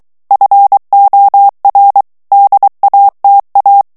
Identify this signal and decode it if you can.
Morse code, heard as a keyed tone, about 23 wpm, 800 Hz. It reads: FOR DATA
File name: ANGLAIS texte clair 3 vitesse 22 mots minute